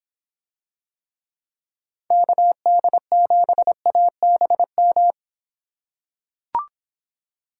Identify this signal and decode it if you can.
Morse code, heard as a keyed tone, about 25 wpm, 700 Hz. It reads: KD7ABM